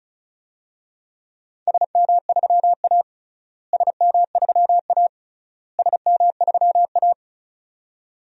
Morse code, heard as a keyed tone, about 34 wpm, 700 Hz. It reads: SM3A SM3A SM3A
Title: DXCC-100 Call Signs - 4 Characters - Repeated 3x - 35wpm